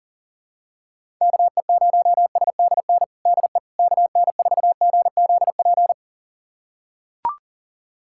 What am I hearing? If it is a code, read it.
Morse code, heard as a keyed tone, about 40 wpm, 700 Hz. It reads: KE0SDN DE KN4GZP
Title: USA Callsign DE Callsign - 6 Characters - 40wpm